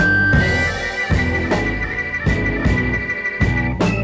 Emergency Alert System (SAME) tones, heard as harsh buzzing bursts. It is heard from 0.5 to 3.3 s.